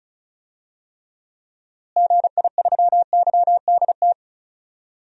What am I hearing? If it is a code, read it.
Morse code, heard as a keyed tone, about 36 wpm, 700 Hz. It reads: GI3YDT
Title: DXCC-100 Call Signs - 6 Characters - Repeated 2x - 35wpm